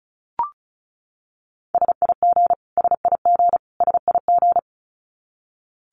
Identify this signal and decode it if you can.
Morse code, heard as a keyed tone, about 34 wpm, 700 Hz. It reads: SIG SIG SIG
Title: Abbreviations Repeated 3x - 35wpm